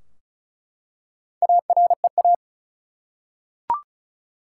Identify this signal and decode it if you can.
Morse code, heard as a keyed tone, about 36 wpm, 700 Hz. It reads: AREA
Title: Four Letter Words - 35wpm